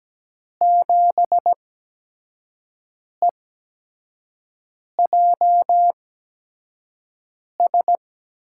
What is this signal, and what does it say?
Morse code, heard as a keyed tone, about 17 wpm, spaced as for 5 wpm, 700 Hz. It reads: 7EJS
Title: USA Call Signs - 6 Characters - 8x Character Spacing - 17wpm